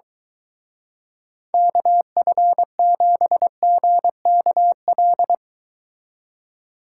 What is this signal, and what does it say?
Morse code, heard as a keyed tone, about 23 wpm, 700 Hz. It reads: KF7GKL